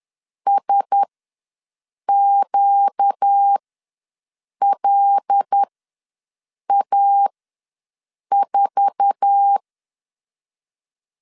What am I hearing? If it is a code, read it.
Morse code, heard as a keyed tone, about 10 wpm, spaced as for 6 wpm, 800 Hz. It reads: SQLA4
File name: LEZIONI / 8 YJ45 / 8 YJ4 ws 10wpm 4020